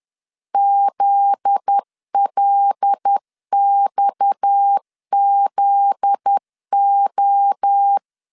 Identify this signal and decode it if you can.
Morse code, heard as a keyed tone, about 10 wpm, 800 Hz. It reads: ZLXZO
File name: LEZIONI / 7 GF9Z / 7 GF9Z 10wpm 2611